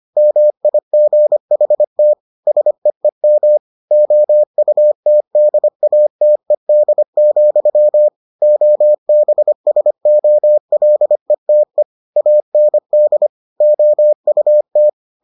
Morse code, heard as a keyed tone, about 25 wpm, 600 Hz. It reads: MIGHT SEEM OUTDATED, OBSOLETE AND OUT